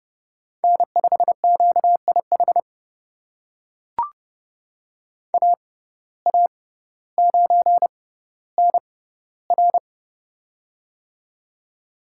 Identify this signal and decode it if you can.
Morse code, heard as a keyed tone, about 30 wpm, spaced as for 15 wpm, 700 Hz. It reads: N5QIH AA9NR
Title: USA Call Signs - 5 Characters - 6x Character Spacing - 30wpm